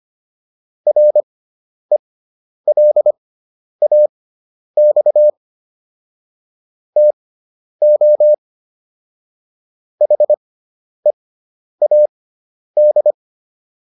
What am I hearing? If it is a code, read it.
Morse code, heard as a keyed tone, about 25 wpm, spaced as for 10 wpm, 600 Hz. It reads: RELAX TO HEAD